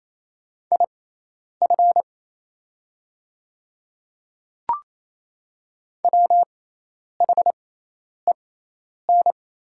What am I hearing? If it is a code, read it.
Morse code, heard as a keyed tone, about 28 wpm, spaced as for 10 wpm, 700 Hz. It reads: IF WHEN
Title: 6x Character Spacing - Top 100 Words - 28wpm